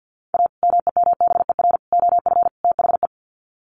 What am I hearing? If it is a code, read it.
Morse code, heard as a keyed tone, about 50 wpm, 700 Hz. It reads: A MEMBER OF THE
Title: Sets of 4 Words 50wpm